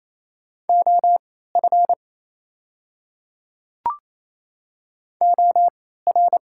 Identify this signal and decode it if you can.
Morse code, heard as a keyed tone, about 28 wpm, spaced as for 16 wpm, 690 Hz. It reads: OF OR
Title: 3x Character Spacing - Top 500 Words - 28wpm